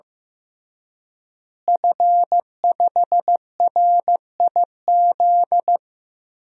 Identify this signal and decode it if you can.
Morse code, heard as a keyed tone, about 15 wpm, 700 Hz. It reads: F5RIZ
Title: DXCC-100 Call Signs - 5 Characters - Repeated 3x - 15wpm